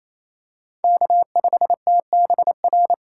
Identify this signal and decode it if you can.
Morse code, heard as a keyed tone, about 28 wpm, 700 Hz. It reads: K5TBR
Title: USA Call Signs - 6x Character Spacing - 28wpm